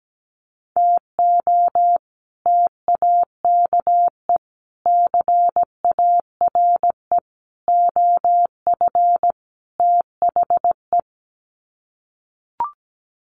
Morse code, heard as a keyed tone, about 17 wpm, 700 Hz. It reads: TO TAKE CARE OF THE
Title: Sets of 5 Words Encore 17wpm